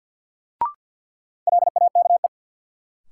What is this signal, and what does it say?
Morse code, heard as a keyed tone, about 52 wpm, 700 Hz. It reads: LAKE
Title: Sentences from Top 1000 Words - Review Part 3 - New Word Drill - 50wpm